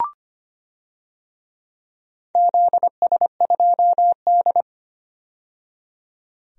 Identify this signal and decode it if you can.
Morse code, heard as a keyed tone, about 25 wpm, 700 Hz. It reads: ZS2D
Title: DXCC-100 Call Signs - 4 Characters - 25wpm